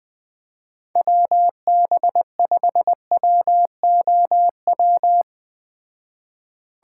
Morse code, heard as a keyed tone, about 20 wpm, 700 Hz. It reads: WB5WOW